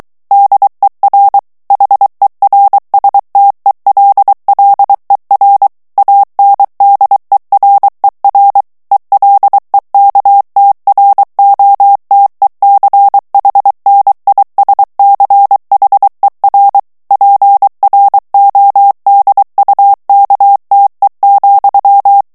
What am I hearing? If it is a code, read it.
Morse code, heard as a keyed tone, about 23 wpm, 800 Hz. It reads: DER HERSTELLER ANDERER ELEKTROTECHNISCHER PRODUKTE,
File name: morse mp3 / ALLEMAND texte clair 1 vitesse 22 mots minute